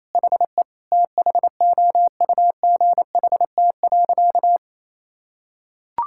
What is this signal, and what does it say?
Morse code, heard as a keyed tone, about 27 wpm, 700 Hz. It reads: HE THOUGHT.
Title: Sentences from Top 1600 Words 28wpm